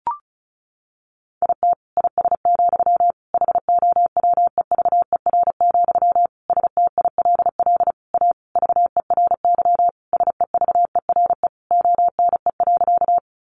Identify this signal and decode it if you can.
Morse code, heard as a keyed tone, about 34 wpm, 700 Hz. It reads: IT IS, HOWEVER, STILL A VERY SEVERE ONE.
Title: Sentences from Top 1400 Words 35wpm